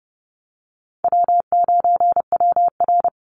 Morse code, heard as a keyed tone, about 30 wpm, 700 Hz. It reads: W9WR
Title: Easy Call Signs Repeated 3x - 30wpm